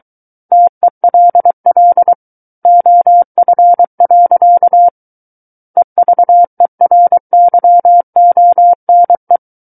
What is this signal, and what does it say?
Morse code, heard as a keyed tone, about 23 wpm, 700 Hz. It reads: TELL OF. EVERYONE
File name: Book One / 14-Chapter 14 - 23 WPM